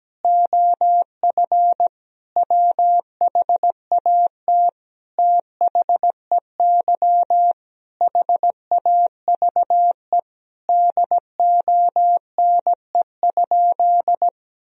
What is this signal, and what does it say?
Morse code, heard as a keyed tone, about 17 wpm, 700 Hz. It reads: OF WHAT THEY HAVE DONE?
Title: Sentences from Top 1000 Words - Review Part 3 - 17wpm